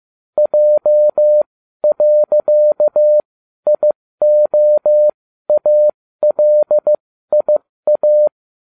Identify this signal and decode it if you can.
Morse code, heard as a keyed tone, about 15 wpm, spaced as for 12 wpm, 600 Hz. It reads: J.IOALIA